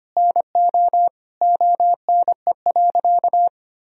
Morse code, heard as a keyed tone, about 25 wpm, 700 Hz. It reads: NO ONE.
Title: Sentences from Top 1000 Words - Review Part 2 - 25wpm